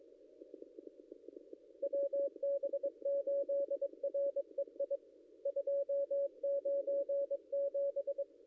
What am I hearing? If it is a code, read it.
Morse code, heard as a keyed tone, about 22 wpm, 560 Hz. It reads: WB8REI 297
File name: contests / ss-cw-k5tr-op-n5rz -0800